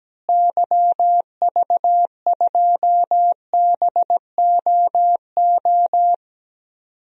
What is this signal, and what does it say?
Morse code, heard as a keyed tone, about 17 wpm, 700 Hz. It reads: YV2BOO
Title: DXCC-100 Call Signs - 6 Characters - Repeated 2x - 17wpm